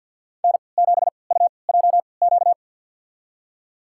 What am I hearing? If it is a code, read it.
Morse code, heard as a keyed tone, about 49 wpm, spaced as for 28 wpm, 700 Hz. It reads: N7UJQ
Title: USA Call Signs - 3x Character Spacing - 50wpm